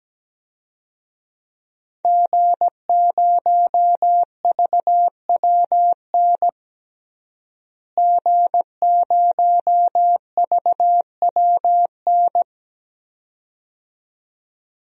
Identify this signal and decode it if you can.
Morse code, heard as a keyed tone, about 17 wpm, 700 Hz. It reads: G0VWN G0VWN
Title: DXCC-100 Call Signs - 5 Characters - Repeated 2x - 17wpm